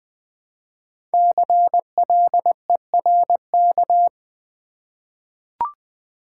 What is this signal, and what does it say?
Morse code, heard as a keyed tone, about 20 wpm, 700 Hz. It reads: CLERK
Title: Five Letter Words - 20wpm